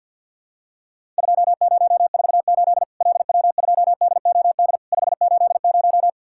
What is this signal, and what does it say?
Morse code, heard as a keyed tone, about 50 wpm, 700 Hz. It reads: 2048 LW2DOD 580
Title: Worked All Europe DX Contest - QTC Exchange - 50wpm